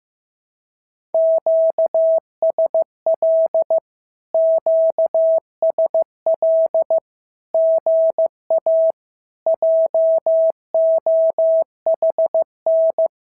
Morse code, heard as a keyed tone, about 15 wpm, 650 Hz. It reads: QSL QSL GA JOHN
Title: SOTA QSO Practice 15wpm